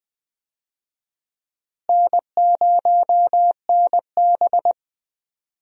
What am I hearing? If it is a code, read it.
Morse code, heard as a keyed tone, about 21 wpm, 700 Hz. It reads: N0NB
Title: USA Call Signs - 4 Characters - Repeated 3x - 20wpm